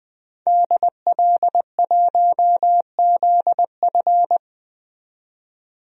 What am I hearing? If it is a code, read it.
Morse code, heard as a keyed tone, about 20 wpm, 700 Hz. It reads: DL1ZF
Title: DXCC-100 Call Signs - 5 Characters - 20wpm